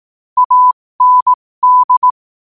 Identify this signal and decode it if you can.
Morse code, heard as a keyed tone, about 18 wpm, spaced as for 15 wpm, 1000 Hz. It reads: AND
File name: qotdfiles / 15wpm